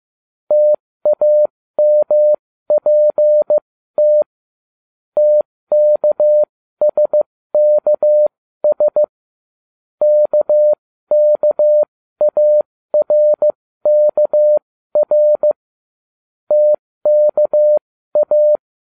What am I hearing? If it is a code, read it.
Morse code, heard as a keyed tone, about 15 wpm, spaced as for 13 wpm, 600 Hz. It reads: TAMPT TKSKS KKARKR TKA